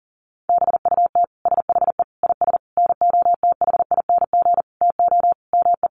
Morse code, heard as a keyed tone, about 40 wpm, 700 Hz. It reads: BUT SHE IS NOTHING TO ME
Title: Sentences from Top 500 Words 40wpm